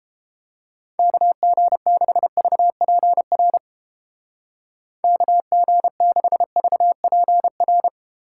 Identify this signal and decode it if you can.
Morse code, heard as a keyed tone, about 29 wpm, 700 Hz. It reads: KG6VPR KG6VPR